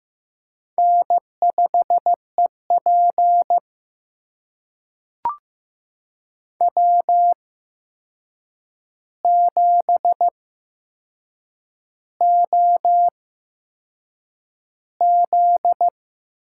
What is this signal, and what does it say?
Morse code, heard as a keyed tone, about 15 wpm, spaced as for 6 wpm, 700 Hz. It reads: N5EP W7OZ
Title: USA Call Signs - 4 Characters - 8x Character Spacing - 15wpm